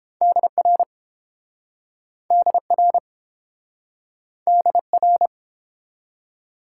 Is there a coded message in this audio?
Yes: Morse code from 0.2 to 5.3 s, heard as a keyed tone, about 31 wpm, 700 Hz. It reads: DR DR DR